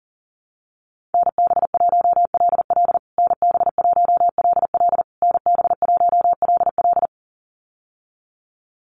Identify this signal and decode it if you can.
Morse code, heard as a keyed tone, about 40 wpm, 700 Hz. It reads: NB1LL NB1LL NB1LL
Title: USA Call Signs Repeated 3x 40wpm